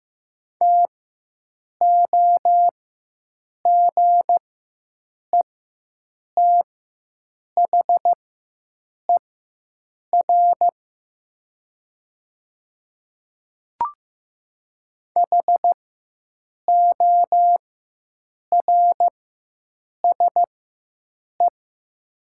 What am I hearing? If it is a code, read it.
Morse code, heard as a keyed tone, about 15 wpm, spaced as for 7 wpm, 700 Hz. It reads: TOGETHER HORSE
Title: 4x Character Spacing - Top 500 Words - 15wpm